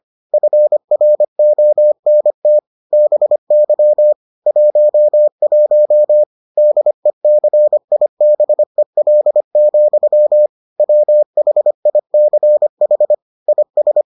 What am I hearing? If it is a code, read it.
Morse code, heard as a keyed tone, about 25 wpm, 600 Hz. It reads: FRONT BY 11 DECIBEL, WHICH IS